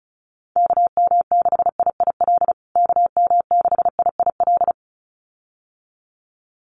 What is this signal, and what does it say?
Morse code, heard as a keyed tone, about 34 wpm, 700 Hz. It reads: KM6IIL KM6IIL
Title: USA Call Signs Repeated 2x 35wpm